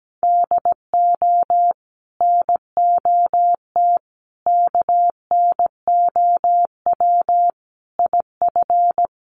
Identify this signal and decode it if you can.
Morse code, heard as a keyed tone, about 17 wpm, 700 Hz. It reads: DO NOT KNOW IF
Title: Sets of 5 Words 17wpm